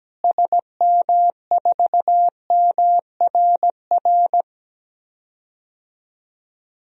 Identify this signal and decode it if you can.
Morse code, heard as a keyed tone, about 17 wpm, 700 Hz. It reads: SM4MRR
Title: DXCC-100 Call Signs - 6 Characters - Repeated 2x - 17wpm